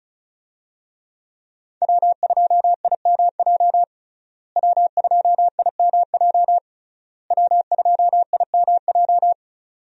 Morse code, heard as a keyed tone, about 36 wpm, 700 Hz. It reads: W2IMJ W2IMJ W2IMJ
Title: USA Call Signs - 5 Characters - Repeated 3x - 35wpm